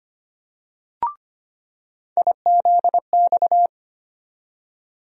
Morse code, heard as a keyed tone, about 25 wpm, 700 Hz. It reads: IZX
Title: USA Call Sign Postfixes - 3 Letters 25wpm